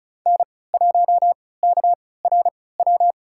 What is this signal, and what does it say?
Morse code, heard as a keyed tone, about 36 wpm, spaced as for 20 wpm, 700 Hz. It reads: N1KRW
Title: USA Call Signs - 5 Characters - 3x Character Spacing - 35wpm